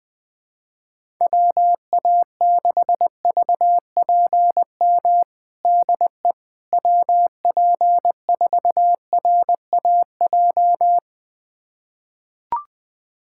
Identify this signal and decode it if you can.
Morse code, heard as a keyed tone, about 20 wpm, 700 Hz. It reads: WA6VPM DE WP4RAJ
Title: USA Callsign DE Callsign - 6 Characters - 20wpm